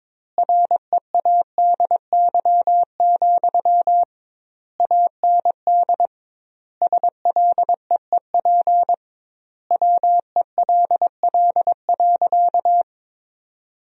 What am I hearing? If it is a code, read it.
Morse code, heard as a keyed tone, about 22 wpm, 700 Hz. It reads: READY, AND SLEEP WELL.
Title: 2x Word-Spacing - Sentences from Top 500 Words - 22wpm